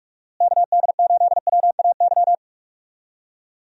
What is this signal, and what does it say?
Morse code, heard as a keyed tone, about 45 wpm, 700 Hz. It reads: KD8WAY
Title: USA Call Signs - 6 Characters - Repeated 2x - 45wpm